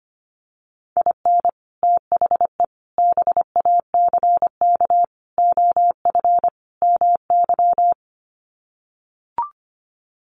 Morse code, heard as a keyed tone, about 25 wpm, 700 Hz. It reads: IN THE BACK OF MY
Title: Sets of 5 Words Encore 25wpm